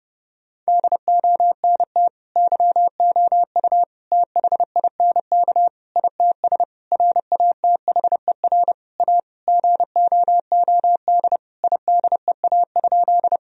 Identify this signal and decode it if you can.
Morse code, heard as a keyed tone, about 30 wpm, 700 Hz. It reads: DONT YOU THINK ITS RATHER A GOOD IDEA?
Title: Sentences from Top 1000 Words - Review Part 4 - 30wpm